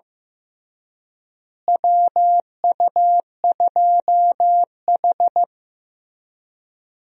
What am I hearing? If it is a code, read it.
Morse code, heard as a keyed tone, about 15 wpm, 700 Hz. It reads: WU2H
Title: USA Call Signs - 4 Characters - Repeated 3x - 15wpm